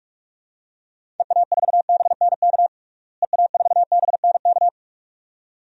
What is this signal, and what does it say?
Morse code, heard as a keyed tone, about 46 wpm, 700 Hz. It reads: EA4BNK EA4BNK
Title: DXCC-100 Call Signs - 6 Characters - Repeated 2x - 45wpm